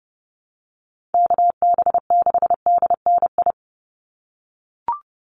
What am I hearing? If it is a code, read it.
Morse code, heard as a keyed tone, about 30 wpm, 700 Hz. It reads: KB6DNI
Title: USA Call Signs 30wpm